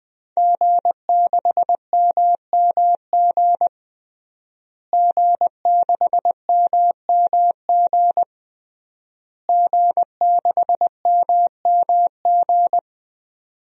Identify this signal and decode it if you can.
Morse code, heard as a keyed tone, about 21 wpm, 700 Hz. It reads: G6MMG G6MMG G6MMG
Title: DXCC-100 Call Signs - 5 Characters - Repeated 3x - 20wpm